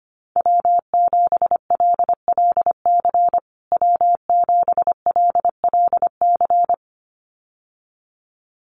Morse code, heard as a keyed tone, about 25 wpm, 700 Hz. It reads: W7LLC W7LLC
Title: USA Call Signs Repeated 2x 25wpm